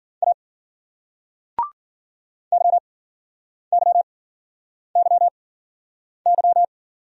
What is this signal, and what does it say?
Morse code, heard as a keyed tone, about 55 wpm, 700 Hz. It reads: A Y Y Y Y